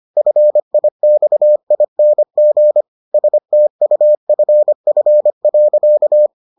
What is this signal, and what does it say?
Morse code, heard as a keyed tone, about 25 wpm, 600 Hz. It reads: FIXING STUFF.